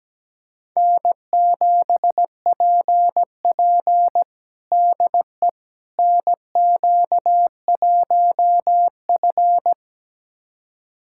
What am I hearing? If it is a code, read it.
Morse code, heard as a keyed tone, about 17 wpm, 700 Hz. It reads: N7PP DE NQ1F
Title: USA Callsign DE Callsign - 4 Characters - 17wpm